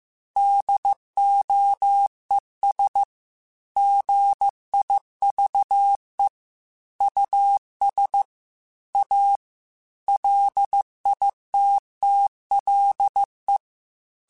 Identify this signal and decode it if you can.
Morse code, heard as a keyed tone, about 15 wpm, 800 Hz. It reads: DOES GIVE US A LITTLE
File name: QSOs / QSOs-mp3s / WD0BGP 2@13